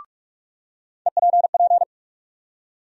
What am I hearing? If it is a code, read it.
Morse code, heard as a keyed tone, about 46 wpm, 700 Hz. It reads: EPP
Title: USA Call Sign Postfixes - 3 Letters 45wpm